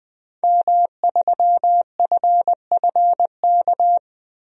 Morse code, heard as a keyed tone, about 20 wpm, 700 Hz. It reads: M3FFK
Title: DXCC-100 Call Signs - 5 Characters - 20wpm